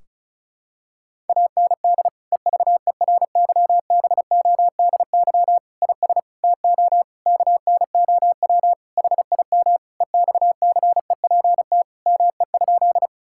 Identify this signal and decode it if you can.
Morse code, heard as a keyed tone, about 36 wpm, 700 Hz. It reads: AND EVERYBODY IS TO KNOW HIM EXCEPT ME?
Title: Sentences from Top 1600 Words 35wpm